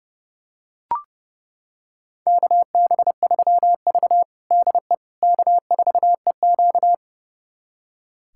Morse code, heard as a keyed tone, about 30 wpm, 700 Hz. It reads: KB3V DE K4EQ
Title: USA Callsign DE Callsign - 4 Characters - 30wpm